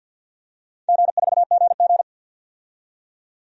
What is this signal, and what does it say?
Morse code, heard as a keyed tone, about 53 wpm, 700 Hz. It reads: G4GG
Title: DXCC-100 Call Signs - 4 Characters - 50wpm